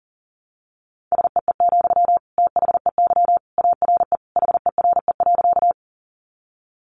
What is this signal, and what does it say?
Morse code, heard as a keyed tone, about 40 wpm, 700 Hz. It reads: SEE, THEY ARE HERE.
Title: Sentences from Top 200 Words 40wpm